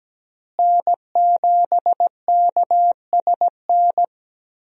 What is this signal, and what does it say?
Morse code, heard as a keyed tone, about 17 wpm, 700 Hz. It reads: N7KSN
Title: USA Call Signs - 5 Characters - Repeated 2x - 17wpm